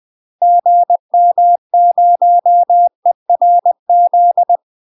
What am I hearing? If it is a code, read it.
Morse code, heard as a keyed tone, about 20 wpm, 700 Hz. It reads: GM0ERZ
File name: UK MASTER QSOs TRAINING / 20WPM / UK MASTER QSOs 0043